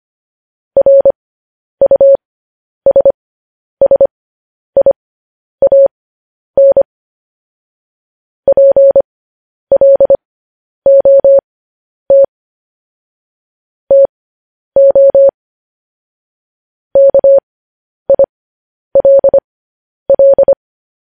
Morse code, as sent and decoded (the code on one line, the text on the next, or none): .-. ..- ... ... .. .- -. / .--. .-.. --- - / - --- / -.- .. .-.. .-..
RUSSIAN PLOT TO KILL